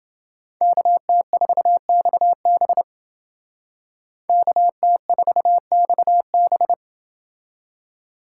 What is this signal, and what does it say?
Morse code, heard as a keyed tone, about 29 wpm, 700 Hz. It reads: KT4XB KT4XB